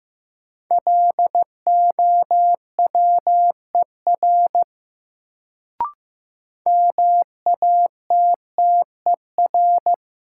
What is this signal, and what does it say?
Morse code, heard as a keyed tone, about 15 wpm, 700 Hz. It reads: LOWER MATTER
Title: Sentences from Top 700 Words - New Word Drill - 15wpm